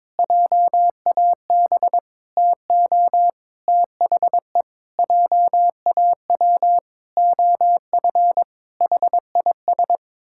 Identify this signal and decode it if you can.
Morse code, heard as a keyed tone, about 22 wpm, 700 Hz. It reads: JAB TO THE JAW OF HIS
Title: Pangram Challenge v2 - 22wpm